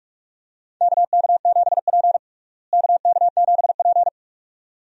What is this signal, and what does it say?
Morse code, heard as a keyed tone, about 46 wpm, 700 Hz. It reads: KK7P KK7P
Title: USA Call Signs - 4 Characters - Repeated 2x - 45wpm